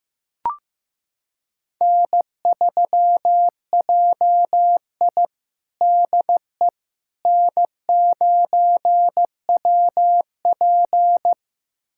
Morse code, heard as a keyed tone, about 16 wpm, 700 Hz. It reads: N3JI DE N9WP
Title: USA Callsign DE Callsign - 4 Characters - 15wpm